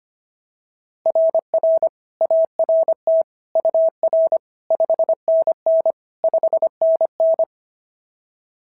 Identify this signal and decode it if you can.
Morse code, heard as a keyed tone, about 25 wpm, 690 Hz. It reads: RR ART UR 5NN 5NN
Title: SOTA QSO Practice 25wpm